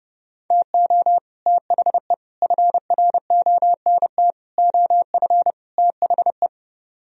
Morse code, heard as a keyed tone, about 30 wpm, 700 Hz. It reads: TO THE FRONT OF THE